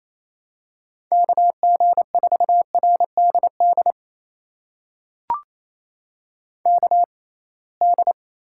Morse code, heard as a keyed tone, about 28 wpm, spaced as for 21 wpm, 700 Hz. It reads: KG4RDD KD
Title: USA Call Signs - 6 Characters - 6x Character Spacing - 28wpm